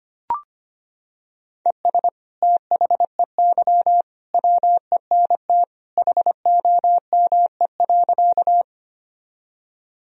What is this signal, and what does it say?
Morse code, heard as a keyed tone, about 25 wpm, 700 Hz. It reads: ES THEY WENT HOME.
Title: Sentences from Top 200 Words and Abbreviations 25wpm